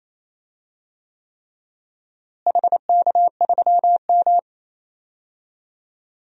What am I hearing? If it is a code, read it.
Morse code, heard as a keyed tone, about 28 wpm, 700 Hz. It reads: HK3M
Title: DXCC-100 Call Signs - 4 Characters - Repeated 3x - 28wpm